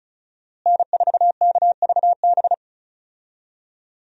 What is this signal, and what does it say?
Morse code, heard as a keyed tone, about 35 wpm, 700 Hz. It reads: N4KVB
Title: USA Call Signs - 5 Characters - Repeated 3x - 35wpm